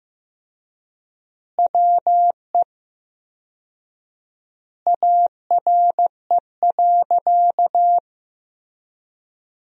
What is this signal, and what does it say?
Morse code, heard as a keyed tone, about 16 wpm, 700 Hz. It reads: WE ARE.
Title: 4x Word-Spacing - Sentences from Top 200 Words - 15wpm